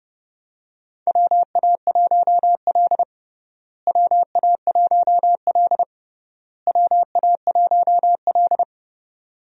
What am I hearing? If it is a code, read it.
Morse code, heard as a keyed tone, about 28 wpm, 700 Hz. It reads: WA1L WA1L WA1L
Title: USA Call Signs - 4 Characters - Repeated 3x - 30wpm